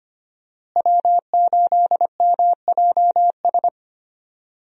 Morse code, heard as a keyed tone, about 25 wpm, 700 Hz. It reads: W8MJS
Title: USA Call Signs - 5 Characters - Repeated 2x - 25wpm